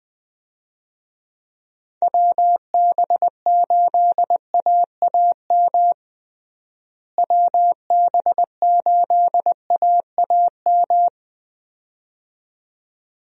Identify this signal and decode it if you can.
Morse code, heard as a keyed tone, about 20 wpm, 700 Hz. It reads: WB8AAM WB8AAM
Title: USA Call Signs - 6 Characters - Repeated 2x - 20wpm